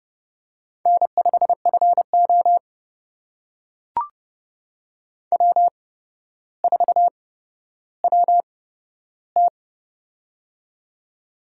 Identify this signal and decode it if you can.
Morse code, heard as a keyed tone, about 30 wpm, spaced as for 13 wpm, 700 Hz. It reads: N5FO W4WT
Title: USA Call Signs - 4 Characters - 8x Character Spacing - 30wpm